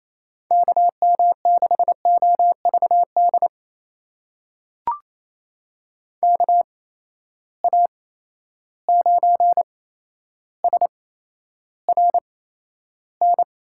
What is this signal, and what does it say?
Morse code, heard as a keyed tone, about 27 wpm, spaced as for 12 wpm, 700 Hz. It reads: KM6OVD KA9SRN
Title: USA Call Signs - 6 Characters - 8x Character Spacing - 28wpm